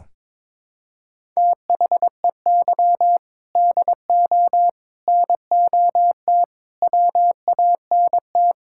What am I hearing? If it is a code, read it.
Morse code, heard as a keyed tone, about 22 wpm, 700 Hz. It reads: THEY DO NOT WANT
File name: practice / 22wpm / Sentences with 4-Word Phrases - 22wpm